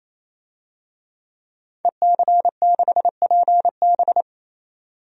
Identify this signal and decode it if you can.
Morse code, heard as a keyed tone, about 28 wpm, 700 Hz. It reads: EC6PB
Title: DXCC-100 Call Signs - 5 Characters - 28wpm